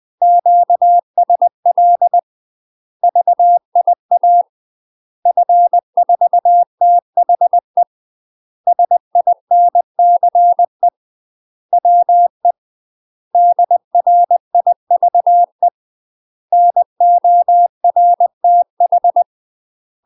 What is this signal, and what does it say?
Morse code, heard as a keyed tone, about 20 wpm, 700 Hz. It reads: QSL VIA F4THE SINCE WE DRIVE NORTH
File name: UK MASTER QSOs TRAINING / 20WPM / UK MASTER QSOs 0032